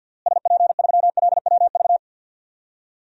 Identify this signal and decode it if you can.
Morse code, heard as a keyed tone, about 53 wpm, 700 Hz. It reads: SP3LWV
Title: DXCC-100 Call Signs - 6 Characters - Repeated 2x - 50wpm